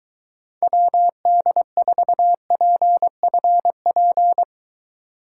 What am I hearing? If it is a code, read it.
Morse code, heard as a keyed tone, about 23 wpm, 700 Hz. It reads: WD4PFP